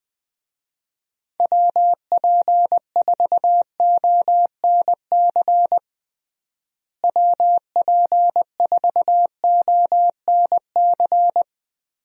Morse code, heard as a keyed tone, about 20 wpm, 700 Hz. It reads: WP4ONC WP4ONC
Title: USA Call Signs - 6 Characters - Repeated 3x - 20wpm